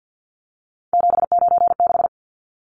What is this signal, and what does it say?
Morse code, heard as a keyed tone, about 50 wpm, 700 Hz. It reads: 796
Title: Numbers - 3-Digit 50wpm